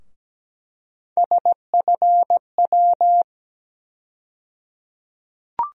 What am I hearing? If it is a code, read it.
Morse code, heard as a keyed tone, about 17 wpm, 700 Hz. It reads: SFW
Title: USA Call Sign Postfixes - 3 Letters 17wpm